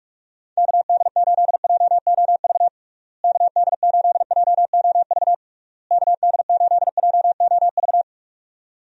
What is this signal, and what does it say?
Morse code, heard as a keyed tone, about 46 wpm, 700 Hz. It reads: KD8JOV KD8JOV KD8JOV
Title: USA Call Signs - 6 Characters - Repeated 3x - 45wpm